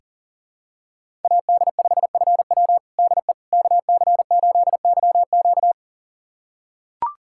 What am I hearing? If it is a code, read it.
Morse code, heard as a keyed tone, about 40 wpm, 700 Hz. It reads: AD5FW DE KC8YQ
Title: USA Callsign DE Callsign - 5 Characters - 40wpm